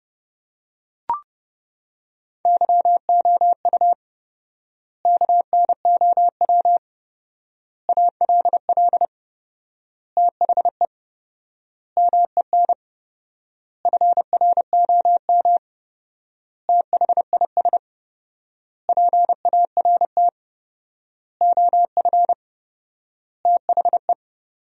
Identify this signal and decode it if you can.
Morse code, heard as a keyed tone, about 30 wpm, 700 Hz. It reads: YOU KNOW ALL THE MEN FROM THIS PART OF THE
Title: 4x Word-Spacing - Sentences from Top 300 Words - 30wpm